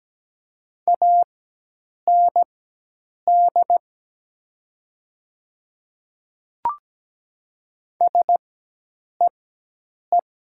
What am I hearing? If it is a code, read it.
Morse code, heard as a keyed tone, about 17 wpm, spaced as for 8 wpm, 700 Hz. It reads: AND SEE
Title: 4x Character Spacing - Top 100 Words - 17wpm